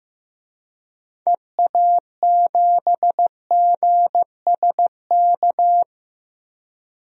Morse code, heard as a keyed tone, about 15 wpm, 700 Hz. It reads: EA7GSK
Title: DXCC-100 Call Signs - 6 Characters - Repeated 3x - 15wpm